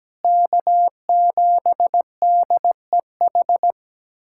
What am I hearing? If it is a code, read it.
Morse code, heard as a keyed tone, about 17 wpm, 700 Hz. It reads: K7DEH
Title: USA Callsign DE Callsign - 5 Characters - 17wpm